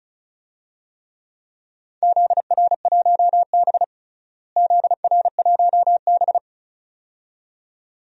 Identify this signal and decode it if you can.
Morse code, heard as a keyed tone, about 34 wpm, 700 Hz. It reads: ZR1B ZR1B
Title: DXCC-100 Call Signs - 4 Characters - Repeated 2x - 35wpm